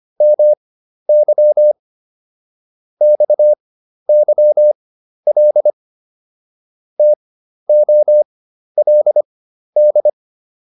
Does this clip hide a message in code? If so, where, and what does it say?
Yes: Morse code from 0.2 to 10.1 s, heard as a keyed tone, about 25 wpm, spaced as for 12 wpm, 600 Hz. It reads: MY XYL TOLD